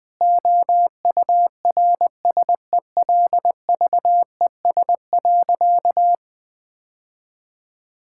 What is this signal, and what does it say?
Morse code, heard as a keyed tone, about 20 wpm, 700 Hz. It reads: OURSELVES.
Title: Difficult Sentences with Hints - 20wpm